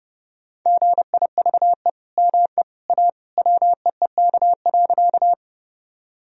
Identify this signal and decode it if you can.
Morse code, heard as a keyed tone, about 29 wpm, 700 Hz. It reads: GIVE ME A WEEK.
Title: Sentences from Top 1000 Words - Review Part 2 - 30wpm